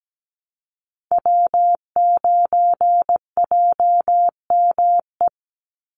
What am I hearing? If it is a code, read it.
Morse code, heard as a keyed tone, about 17 wpm, 700 Hz. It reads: W9JME
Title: USA Call Signs 17wpm